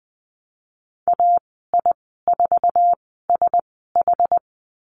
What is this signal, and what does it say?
Morse code, heard as a keyed tone, about 20 wpm, spaced as for 15 wpm, 700 Hz. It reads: AI4SH
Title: 069 Lesson 12g - Letter d - Call Signs 20wpm